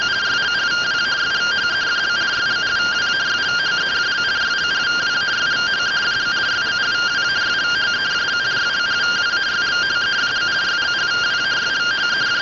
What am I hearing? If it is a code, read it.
Two-tone teleprinter signal, about 170 Hz shift.